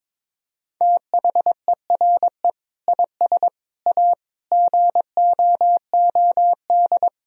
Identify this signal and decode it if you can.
Morse code, heard as a keyed tone, about 22 wpm, 700 Hz. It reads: THERE IS A GOOD
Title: Sentences from Top 1000 Words - Review Part 1 - 22wpm